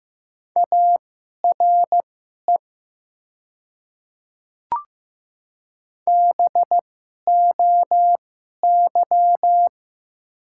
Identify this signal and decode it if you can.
Morse code, heard as a keyed tone, about 15 wpm, spaced as for 11 wpm, 700 Hz. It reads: ARE BOY
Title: 2x Character Spacing - Top 300 Words - 15wpm